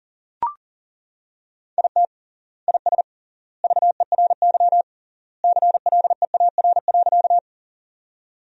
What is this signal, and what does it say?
Morse code, heard as a keyed tone, about 40 wpm, 700 Hz. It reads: IT IS VERY CLEAR.
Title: 3x Word-Spacing - Sentences from Top 500 Words - 40wpm